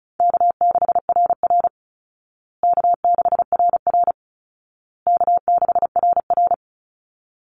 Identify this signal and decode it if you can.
Morse code, heard as a keyed tone, about 34 wpm, 700 Hz. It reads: K6RR K6RR K6RR
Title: Call Signs Repeated 2x - 35wpm